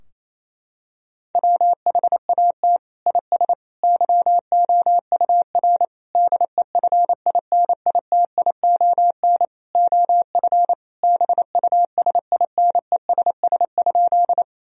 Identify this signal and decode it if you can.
Morse code, heard as a keyed tone, about 28 wpm, 700 Hz. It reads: WHAT IS YOUR DEFINITION OF BUSINESS?
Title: Sentences from Top 1000 Words - Review Part 4 - 28wpm